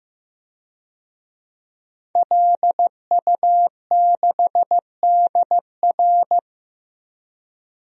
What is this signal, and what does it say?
Morse code, heard as a keyed tone, about 15 wpm, 700 Hz. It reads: LU6DR
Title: DXCC-100 Call Signs - 5 Characters - 15wpm